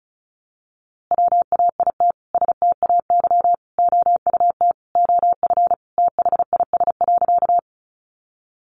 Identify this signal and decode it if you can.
Morse code, heard as a keyed tone, about 34 wpm, 700 Hz. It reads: WAIT STAY OUT OF THIS.
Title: A Sound Of Thunder 35wpm